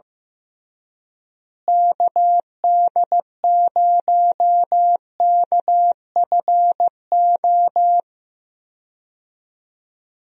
Morse code, heard as a keyed tone, about 15 wpm, 700 Hz. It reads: KD0KFO
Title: USA Call Signs - 6 Characters - Repeated 3x - 15wpm